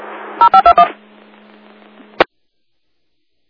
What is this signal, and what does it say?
DTMF, heard as telephone touch tones, tones of about 90 ms. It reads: 0521
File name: Inlet Fire 12 09 14